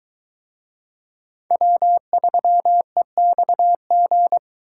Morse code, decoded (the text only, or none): W3EXG